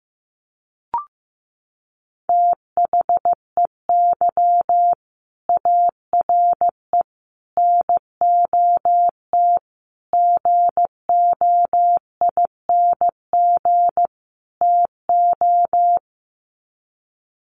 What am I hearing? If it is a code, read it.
Morse code, heard as a keyed tone, about 15 wpm, 700 Hz. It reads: THEY ARE NOT GOING TO
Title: Sets of 5 Words Encore2 15wpm